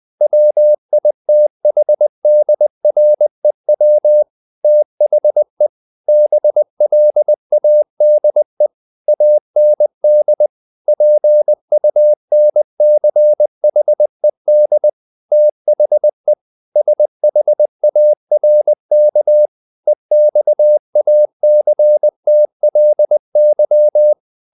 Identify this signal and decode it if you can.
Morse code, heard as a keyed tone, about 20 wpm, 600 Hz. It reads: WITHDREW THE BLADE AND PUNCHED THE SHARK EXACTLY
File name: the old man and the sea part 2 20wpm